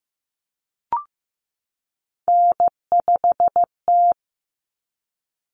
Morse code, heard as a keyed tone, about 15 wpm, 700 Hz. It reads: N5T
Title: Common Cut Numbers - 3-Digit 15wpm